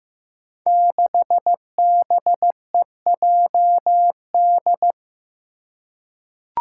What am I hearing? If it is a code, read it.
Morse code, heard as a keyed tone, about 15 wpm, 700 Hz. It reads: 6BEJD
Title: Five Challenging Letter-Number Combinations 15wpm